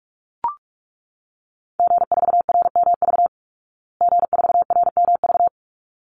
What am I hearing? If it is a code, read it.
Morse code, heard as a keyed tone, about 44 wpm, 700 Hz. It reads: G4RMV G4RMV
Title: Call Signs Repeated 3x - 45wpm